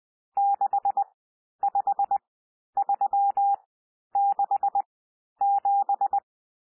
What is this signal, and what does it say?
Morse code, heard as a keyed tone, about 20 wpm, spaced as for 11 wpm, 800 Hz. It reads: 65367
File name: cwmp3 / N1 SOUND NUMBERS 20 wpm